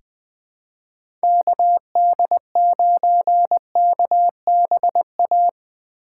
Morse code, heard as a keyed tone, about 20 wpm, 700 Hz. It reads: KD9KBA
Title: USA Call Signs - 6 Characters - 6x Character Spacing - 20wpm